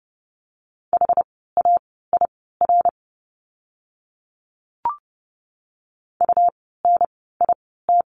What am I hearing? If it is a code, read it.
Morse code, heard as a keyed tone, about 30 wpm, spaced as for 17 wpm, 700 Hz. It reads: HAIR UNIT
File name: Lesson 17f - Letter u - Words 30wpm